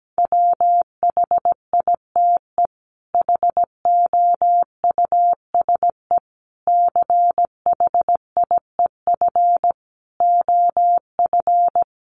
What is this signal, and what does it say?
Morse code, heard as a keyed tone, about 17 wpm, 700 Hz. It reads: WHITE HOUSE CHIEF OF
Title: Sets of 5 Words Encore 17wpm